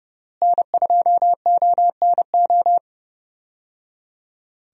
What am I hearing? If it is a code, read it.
Morse code, heard as a keyed tone, about 31 wpm, 700 Hz. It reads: N2ONO
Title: USA Call Signs - 5 Characters - Repeated 3x - 30wpm